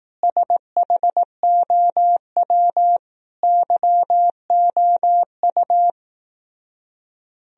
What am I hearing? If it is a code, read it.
Morse code, heard as a keyed tone, about 18 wpm, 690 Hz. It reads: SHOW YOU
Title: Speed-Racing - Sets of 5 Words 15wpm